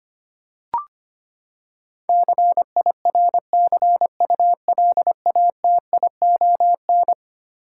Morse code, heard as a keyed tone, about 25 wpm, 700 Hz. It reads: CIRCULATION
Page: morse-code-ninja-practice-25wpm